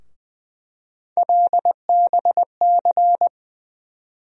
Morse code, heard as a keyed tone, about 20 wpm, 700 Hz. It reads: LBC
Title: USA Call Sign Postfixes - 3 Letters 20wpm